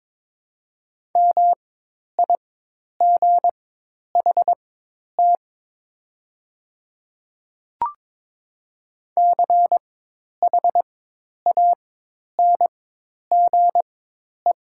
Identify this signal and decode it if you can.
Morse code, heard as a keyed tone, about 22 wpm, spaced as for 10 wpm, 700 Hz. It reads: MIGHT CHANGE
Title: 4x Character Spacing - Top 300 Words - 22wpm